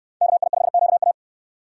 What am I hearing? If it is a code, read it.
Morse code, heard as a keyed tone, about 69 wpm, 700 Hz. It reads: KI5QA